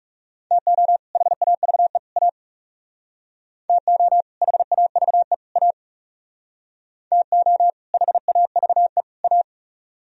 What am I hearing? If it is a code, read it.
Morse code, heard as a keyed tone, about 37 wpm, 700 Hz. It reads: TO HAVE A TO HAVE A TO HAVE A
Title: Speed-Racing - Sets of 3 Words 30wpm